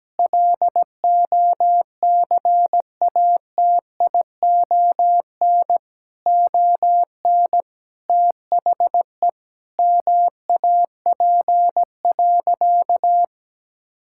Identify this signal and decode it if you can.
Morse code, heard as a keyed tone, about 17 wpm, 700 Hz. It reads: LOCATION ON THE MAP.
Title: Sentences from Top 1000 Words - Review Part 6 - 17wpm